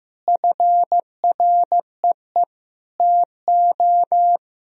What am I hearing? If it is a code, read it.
Morse code, heard as a keyed tone, about 15 wpm, 700 Hz. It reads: FREE TO
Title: Sentences from Top 1000 Words - Review Part 1 - 15wpm